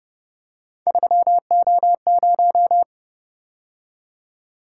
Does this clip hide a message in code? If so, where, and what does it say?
Yes: Morse code from 0.9 to 2.8 s, heard as a keyed tone, about 29 wpm, 700 Hz. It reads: 3O0